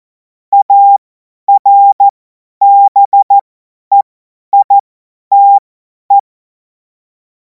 Morse code, heard as a keyed tone, about 14 wpm, spaced as for 10 wpm, 800 Hz. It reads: ARBEITE
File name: Morsekurs / Witze / Witz 10. März 2026 50 F70